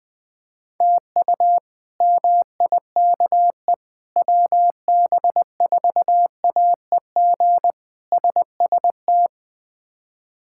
Morse code, heard as a keyed tone, about 20 wpm, 700 Hz. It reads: TU MIKE WB4AEG SST
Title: K1USN SST QSO Practice - 20wpm